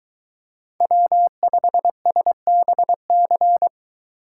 Morse code, heard as a keyed tone, about 23 wpm, 700 Hz. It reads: W5SBC